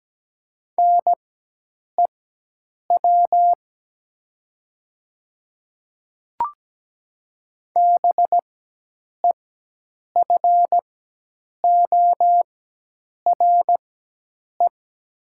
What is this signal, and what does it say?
Morse code, heard as a keyed tone, about 17 wpm, spaced as for 8 wpm, 700 Hz. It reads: NEW BEFORE
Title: 4x Character Spacing - Top 200 Words - 17wpm